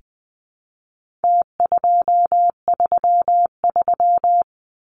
Morse code, heard as a keyed tone, about 20 wpm, 700 Hz. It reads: T233
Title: Numbers - Contest Style - 4-Digit 20wpm